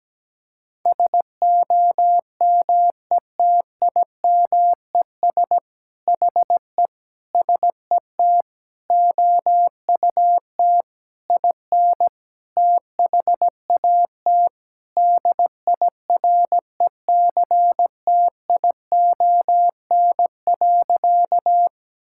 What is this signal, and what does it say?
Morse code, heard as a keyed tone, about 17 wpm, 700 Hz. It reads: SOMETIMES HE SET OUT IN THAT DIRECTION.
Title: Sentences from Top 1000 Words - Review Part 5 - 17wpm